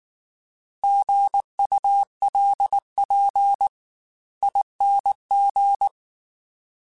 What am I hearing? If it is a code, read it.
Morse code, heard as a keyed tone, about 19 wpm, 800 Hz. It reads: GULP ING